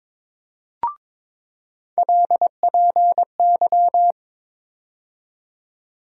Morse code, heard as a keyed tone, about 21 wpm, 700 Hz. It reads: LPY
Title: USA Call Sign Postfixes - 3 Letters 22wpm